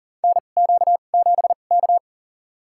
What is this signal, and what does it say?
Morse code, heard as a keyed tone, about 37 wpm, spaced as for 28 wpm, 700 Hz. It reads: NQ7K
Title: USA Call Signs - 4 Characters - 2x Character Spacing - 40wpm